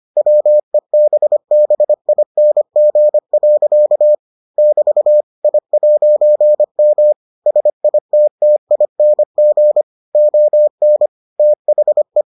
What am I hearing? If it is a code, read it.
Morse code, heard as a keyed tone, about 25 wpm, 600 Hz. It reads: WEBBING. <BT> I'M SITTING ON THE